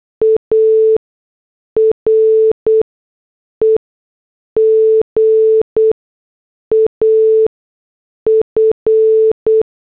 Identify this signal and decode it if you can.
Morse code, heard as a keyed tone, about 8 wpm, spaced as for 6 wpm, 440 Hz. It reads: AREGAF
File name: morse / headlines